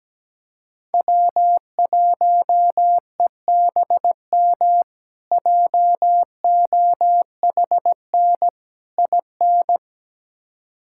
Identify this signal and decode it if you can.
Morse code, heard as a keyed tone, about 17 wpm, 700 Hz. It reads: W1EBM JOHN IN
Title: K1USN SST QSO Practice - 17wpm